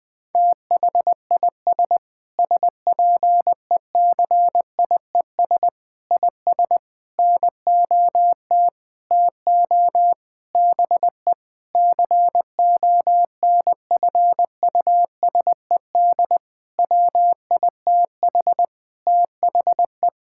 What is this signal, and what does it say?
Morse code, heard as a keyed tone, about 20 wpm, 700 Hz. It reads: THIS SPECIES IS NOT TO BE CONFUSED WITH THE
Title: Sentences with 6-Word Phrases - Spoken Beginning - 20wpm